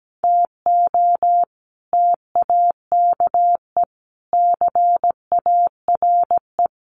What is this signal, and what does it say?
Morse code, heard as a keyed tone, about 17 wpm, 700 Hz. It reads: TO TAKE CARE
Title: Sets of 5 Words Encore 17wpm